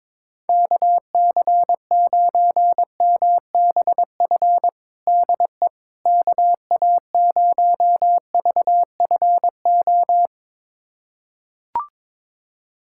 Morse code, heard as a keyed tone, about 22 wpm, 700 Hz. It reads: KC9MBF DE KA0VFO
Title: USA Callsign DE Callsign - 6 Characters - 22wpm